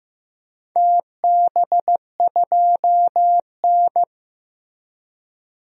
Morse code, heard as a keyed tone, about 15 wpm, 700 Hz. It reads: TB2N
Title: DXCC-100 Call Signs - 4 Characters - Repeated 3x - 15wpm